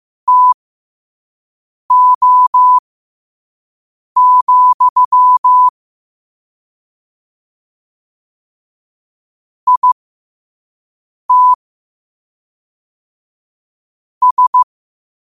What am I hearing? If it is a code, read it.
Morse code, heard as a keyed tone, about 15 wpm, spaced as for 5 wpm, 1000 Hz. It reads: TO, IT S
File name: 5wpm